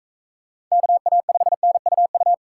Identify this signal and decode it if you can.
Morse code, heard as a keyed tone, about 42 wpm, 700 Hz. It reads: KA5NUU